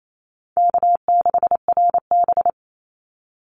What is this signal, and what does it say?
Morse code, heard as a keyed tone, about 28 wpm, 700 Hz. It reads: K6RB
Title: Call Signs Repeated 2x - 28wpm